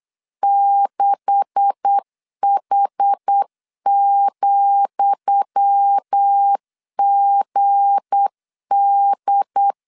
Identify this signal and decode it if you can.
Morse code, heard as a keyed tone, about 8 wpm, 800 Hz. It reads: 6H,GD
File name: LEZIONI / 10 SIGNS / 10D COMA 8wpm 3749